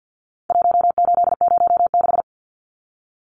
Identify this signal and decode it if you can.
Morse code, heard as a keyed tone, about 50 wpm, 700 Hz. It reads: 1806
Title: Numbers - 4-Digit 50wpm